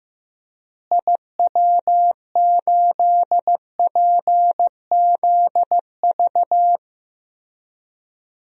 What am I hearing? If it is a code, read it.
Morse code, heard as a keyed tone, about 15 wpm, 700 Hz. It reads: IW8PZV
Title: DXCC-100 Call Signs - 6 Characters - Repeated 3x - 15wpm